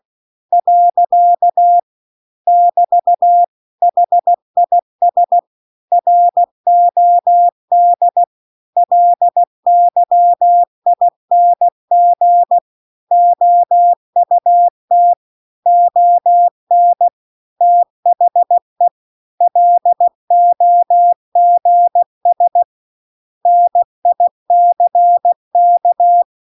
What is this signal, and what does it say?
Morse code, heard as a keyed tone, about 16 wpm, 700 Hz. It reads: . <BT> HIS ROD LYING OUT ON THE LOGS NICK